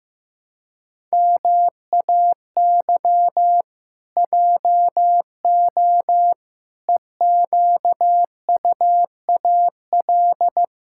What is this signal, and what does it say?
Morse code, heard as a keyed tone, about 15 wpm, 700 Hz. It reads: MAY JO EQUAL
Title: Pangram Challenge v2 - 15wpm